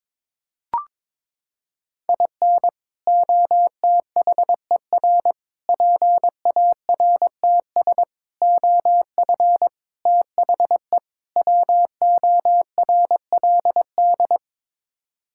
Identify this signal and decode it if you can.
Morse code, heard as a keyed tone, about 22 wpm, 700 Hz. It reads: IN OTHER PARTS OF THE WORLD
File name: practice / 22wpm / Sets of 6 Words - 22wpm